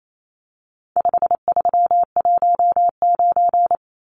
Morse code, heard as a keyed tone, about 28 wpm, 700 Hz. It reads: 5319
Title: Numbers - 4-Digit 28wpm